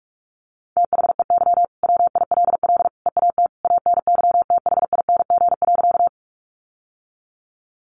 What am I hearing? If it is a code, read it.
Morse code, heard as a keyed tone, about 44 wpm, 700 Hz. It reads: THEY WILL EAT ANYTHING.
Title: Sentences from Top 1000 Words 45wpm